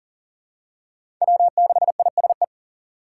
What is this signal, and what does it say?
Morse code, heard as a keyed tone, about 40 wpm, 700 Hz. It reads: W6ISE
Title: USA Call Signs - 5 Characters - 4x Character Spacing - 40wpm